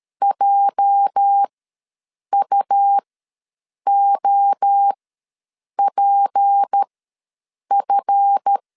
Morse code, heard as a keyed tone, about 13 wpm, spaced as for 7 wpm, 800 Hz. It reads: JUOPF
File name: LEZIONI / 8 YJ45 / 8 YJ ws 12wpm 3342